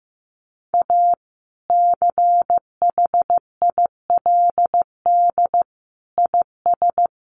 Morse code, heard as a keyed tone, about 15 wpm, 700 Hz. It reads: A CHILD IS
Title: Sentences from Top 1400 Words 15wpm